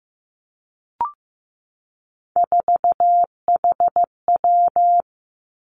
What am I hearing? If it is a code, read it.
Morse code, heard as a keyed tone, about 15 wpm, 700 Hz. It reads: 4HW
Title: Three Challenging Letter-Number Combinations 15wpm